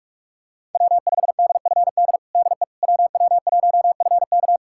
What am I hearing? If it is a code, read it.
Morse code, heard as a keyed tone, about 46 wpm, 700 Hz. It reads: W5DFD DE WW1FK
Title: USA Callsign DE Callsign - 5 Characters - 45wpm